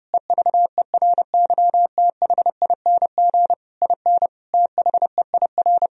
Morse code, heard as a keyed tone, about 32 wpm, 700 Hz. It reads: EVERYTHING IN THEIR
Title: Sentences from Top 1000 Words - Review Part 2 - 30wpm